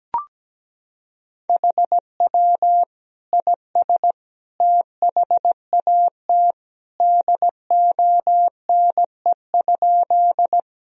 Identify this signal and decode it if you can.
Morse code, heard as a keyed tone, about 17 wpm, 700 Hz. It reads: HW IS THAT DONE?
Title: Sentences from Top 500 Words and Abbreviations 17wpm